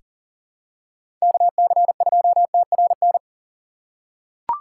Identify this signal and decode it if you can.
Morse code, heard as a keyed tone, about 40 wpm, 700 Hz. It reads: KC2TRN
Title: USA Call Signs - 6x Character Spacing - 40wpm